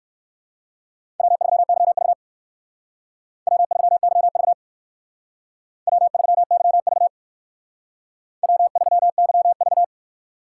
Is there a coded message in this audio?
Yes: Morse code from 1.2 to 9.8 s, heard as a keyed tone, about 59 wpm, 700 Hz. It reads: W3YV W3YV W3YV W3YV